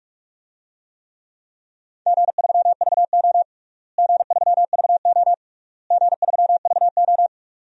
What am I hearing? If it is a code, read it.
Morse code, heard as a keyed tone, about 42 wpm, 700 Hz. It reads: G3VO G3VO G3VO
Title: DXCC-100 Call Signs - 4 Characters - Repeated 3x - 45wpm